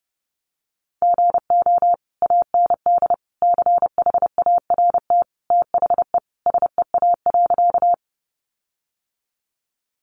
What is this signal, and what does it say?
Morse code, heard as a keyed tone, about 30 wpm, 700 Hz. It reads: GO AND CHART THE SEA.
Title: Sentences from Top 1300 Words 30wpm